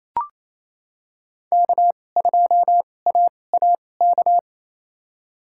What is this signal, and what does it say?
Morse code, heard as a keyed tone, about 28 wpm, spaced as for 20 wpm, 700 Hz. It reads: K2AAK
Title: USA Call Signs - 5 Characters - 2x Character Spacing - 28wpm